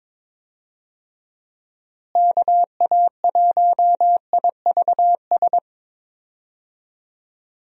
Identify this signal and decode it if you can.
Morse code, heard as a keyed tone, about 22 wpm, 700 Hz. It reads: KA1IVS
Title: USA Call Signs - 6 Characters - Repeated 2x - 22wpm